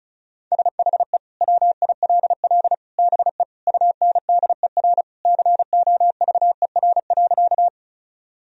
Morse code, heard as a keyed tone, about 34 wpm, 700 Hz. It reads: SHE WILL BE UNDER COVER.
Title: Sentences from Top 1000 Words - Review Part 1 - 35wpm